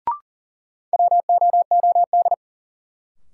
Morse code, heard as a keyed tone, about 40 wpm, 700 Hz. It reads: WOOD
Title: Sentences from Top 1100 Words - New Word Drill Alt - 40wpm